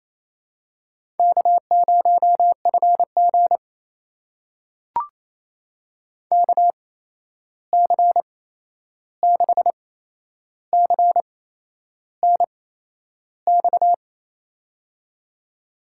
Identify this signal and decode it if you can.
Morse code, heard as a keyed tone, about 28 wpm, spaced as for 11 wpm, 700 Hz. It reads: K0FG KC6CNX
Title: USA Call Signs - 8x Character Spacing - 28wpm